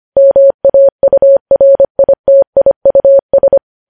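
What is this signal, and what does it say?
Morse code, heard as a keyed tone, about 25 wpm, 550 Hz. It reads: MAURITIUS
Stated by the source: News headlines sent at 25.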